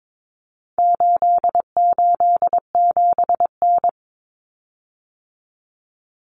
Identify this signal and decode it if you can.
Morse code, heard as a keyed tone, about 22 wpm, 700 Hz. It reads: 887N
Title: Common Cut Numbers - 4-Digit 22wpm